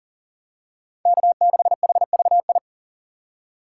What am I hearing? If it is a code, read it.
Morse code, heard as a keyed tone, about 39 wpm, 700 Hz. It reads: K6HVI
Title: USA Call Signs - 5 Characters - 3x Character Spacing - 40wpm